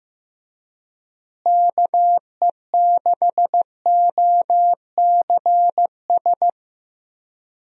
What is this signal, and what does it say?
Morse code, heard as a keyed tone, about 15 wpm, 700 Hz. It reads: KE6OCS
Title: USA Call Signs - 6 Characters - Repeated 3x - 15wpm